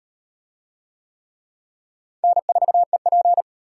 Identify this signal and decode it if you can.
Morse code, heard as a keyed tone, about 34 wpm, 700 Hz. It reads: N4EP